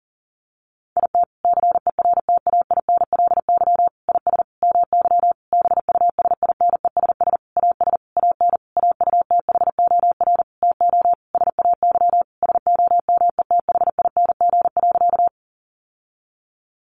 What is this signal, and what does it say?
Morse code, heard as a keyed tone, about 40 wpm, 700 Hz. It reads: IT CERTAINLY IS MY BUSINESS AS AN AUTHOR TO SAY SOMETHING.
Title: Sentences from Top 1200 Words 40wpm